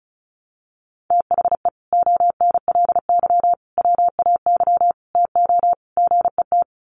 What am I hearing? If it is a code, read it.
Morse code, heard as a keyed tone, about 34 wpm, 700 Hz. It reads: THE ONLY WAY TO GET
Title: Sets of 5 Words Encore2 35wpm